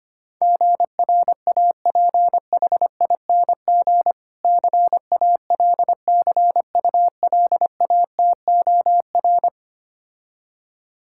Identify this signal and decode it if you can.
Morse code, heard as a keyed tone, about 25 wpm, 700 Hz. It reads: GRAPHING CALCULATOR
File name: practice / 25wpm / Difficult Words - Speed-Racing with Hints - 25wpm